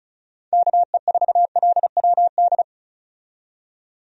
Morse code, heard as a keyed tone, about 36 wpm, 700 Hz. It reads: KE4LWD
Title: USA Call Signs - 6 Characters - 35wpm